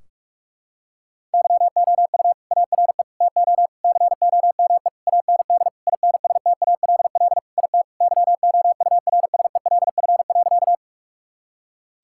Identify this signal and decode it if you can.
Morse code, heard as a keyed tone, about 41 wpm, 700 Hz. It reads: YOU ARE TO COME AND INSTALL IT YOURSELF.
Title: Sentences from Top 1600 Words 45wpm